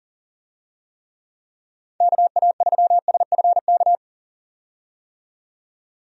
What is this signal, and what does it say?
Morse code, heard as a keyed tone, about 40 wpm, 700 Hz. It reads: KA3SFK
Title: USA Call Signs - 6 Characters - 40wpm